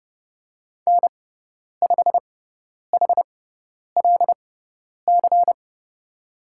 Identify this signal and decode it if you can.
Morse code, heard as a keyed tone, about 30 wpm, spaced as for 10 wpm, 700 Hz. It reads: N5HLC
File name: Lesson 16g - Letter c - Call Signs 30wpm